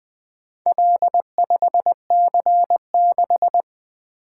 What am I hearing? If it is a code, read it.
Morse code, heard as a keyed tone, about 20 wpm, 700 Hz. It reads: L5C6
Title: Four Letter-Number Combinations - 20wpm